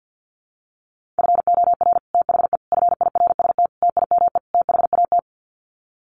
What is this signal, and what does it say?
Morse code, heard as a keyed tone, about 50 wpm, 700 Hz. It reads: FOR THE FIRST TIME THAT
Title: Sets of 5 Words Encore 50wpm